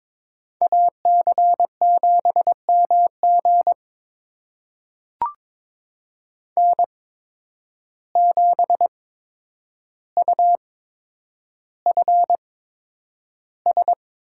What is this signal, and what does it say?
Morse code, heard as a keyed tone, about 22 wpm, spaced as for 9 wpm, 700 Hz. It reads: AC7MG N7UFS
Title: USA Call Signs - 5 Characters - 8x Character Spacing - 22wpm